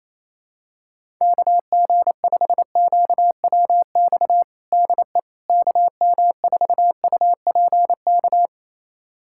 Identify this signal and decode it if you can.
Morse code, heard as a keyed tone, about 28 wpm, 700 Hz. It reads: KG5QWX DE KM4UPK
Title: USA Callsign DE Callsign - 6 Characters - 28wpm